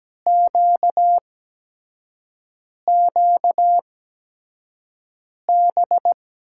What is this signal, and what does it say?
Morse code, heard as a keyed tone, about 17 wpm, spaced as for 5 wpm, 700 Hz. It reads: QQB
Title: USA Call Signs - 6 Characters - 8x Character Spacing - 17wpm